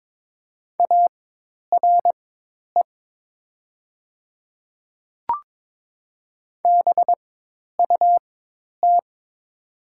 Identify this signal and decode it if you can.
Morse code, heard as a keyed tone, about 21 wpm, spaced as for 10 wpm, 700 Hz. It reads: ARE BUT
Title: 4x Character Spacing - Top 300 Words - 22wpm